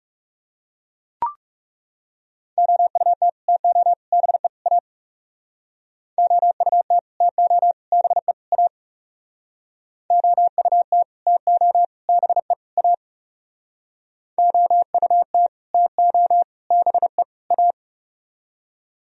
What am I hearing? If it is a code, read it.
Morse code, heard as a keyed tone, about 39 wpm, 700 Hz. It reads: OUT TO BE A OUT TO BE A OUT TO BE A OUT TO BE A